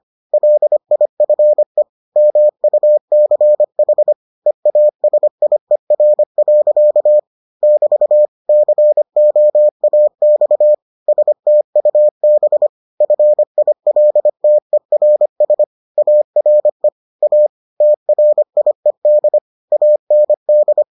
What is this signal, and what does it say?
Morse code, heard as a keyed tone, about 25 wpm, 600 Hz. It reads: LIFE MUCH EASIER. <BT> COAX STUB FILTERS ARE A TRIED AND